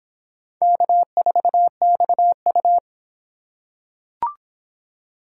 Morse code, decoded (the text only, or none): K4XU